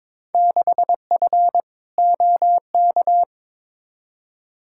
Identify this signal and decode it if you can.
Morse code, heard as a keyed tone, about 22 wpm, 700 Hz. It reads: 6F OK
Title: Field Day Exchanges Encore - 22wpm